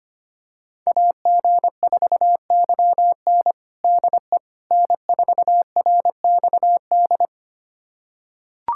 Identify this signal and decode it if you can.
Morse code, heard as a keyed tone, about 25 wpm, 700 Hz. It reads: AG4YN DE N4RXD
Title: USA Callsign DE Callsign - 5 Characters - 25wpm